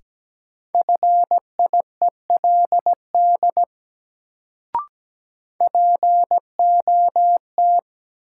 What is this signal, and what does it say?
Morse code, heard as a keyed tone, about 17 wpm, 700 Hz. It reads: FIELD POT
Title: Sentences from Top 1000 Words - New Word Drill - 17wpm